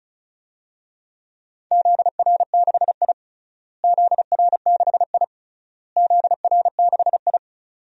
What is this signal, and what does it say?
Morse code, heard as a keyed tone, about 34 wpm, 700 Hz. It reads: ZR6I ZR6I ZR6I
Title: DXCC-100 Call Signs - 4 Characters - Repeated 3x - 35wpm